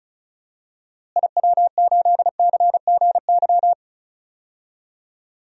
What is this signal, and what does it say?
Morse code, heard as a keyed tone, about 34 wpm, 700 Hz. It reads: IW8CGY
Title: DXCC-100 Call Signs - 6 Characters - Repeated 3x - 35wpm